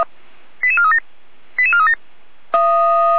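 Five-tone selective call (ZVEI), heard as quick runs of five tones, tones of about 70 ms. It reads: 88437 88437
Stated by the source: ZVEI five-tone call received by SDR.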